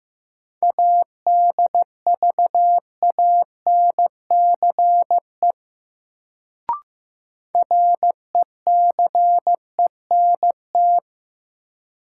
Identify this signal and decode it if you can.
Morse code, heard as a keyed tone, about 15 wpm, 700 Hz. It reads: ADVANCE RECENT
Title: Sentences from Top 1400 Words - New Word Drill Alt - 15wpm